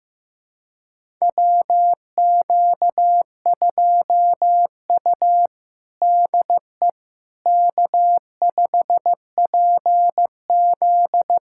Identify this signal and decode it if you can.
Morse code, heard as a keyed tone, about 16 wpm, 700 Hz. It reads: WQ2U DE K5PZ
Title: USA Callsign DE Callsign - 4 Characters - 15wpm